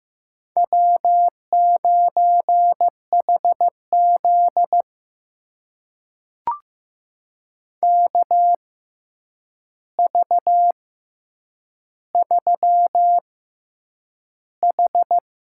Morse code, heard as a keyed tone, about 15 wpm, spaced as for 8 wpm, 700 Hz. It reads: W9HZ KV3H
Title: USA Call Signs - 4 Characters - 6x Character Spacing - 15wpm